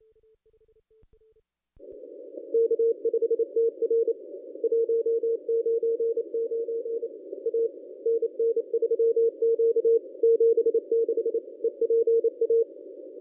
Morse code, heard as a keyed tone, about 28 wpm, 460 Hz. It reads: K5TR 199 A NN3Q 76 EPA
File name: contests / ss-cw-k5tr-op-n5rz -1000